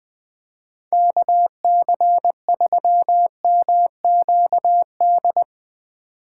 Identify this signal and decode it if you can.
Morse code, heard as a keyed tone, about 22 wpm, 700 Hz. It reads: KC3MQD
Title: Speed Racing - USA Call Signs 20wpm